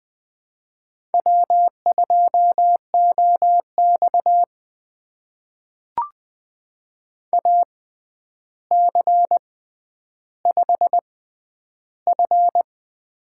Morse code, heard as a keyed tone, about 20 wpm, spaced as for 10 wpm, 700 Hz. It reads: W2OX AC5F
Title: USA Call Signs - 4 Characters - 6x Character Spacing - 20wpm